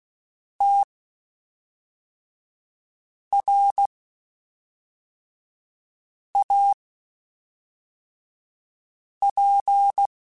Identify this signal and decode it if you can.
Morse code, heard as a keyed tone, about 16 wpm, spaced as for 3 wpm, 800 Hz. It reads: TRAP
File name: (spaced) / QSOs Spaced-mp3s / WD0FBT 1 @5 sp